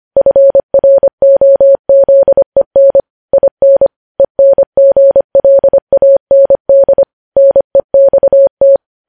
Morse code, heard as a keyed tone, about 25 wpm, 550 Hz. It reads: FROZEN IN ENGLAND NEXT
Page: News Headlines in Morse Code at 25 WPM